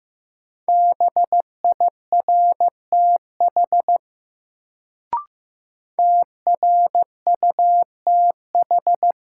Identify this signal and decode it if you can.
Morse code, heard as a keyed tone, about 15 wpm, 700 Hz. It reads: BIRTH TRUTH
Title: Sentences from Top 800 Words - New Word Drill - 15wpm